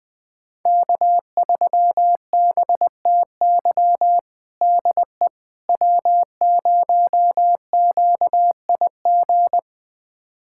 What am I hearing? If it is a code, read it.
Morse code, heard as a keyed tone, about 21 wpm, 700 Hz. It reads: K3BTY DE W0QIG
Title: USA Callsign DE Callsign - 5 Characters - 20wpm